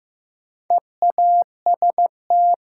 Morse code, heard as a keyed tone, about 15 wpm, 700 Hz. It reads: EAST
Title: Sets of 3 Words - New Word Drill - 15wpm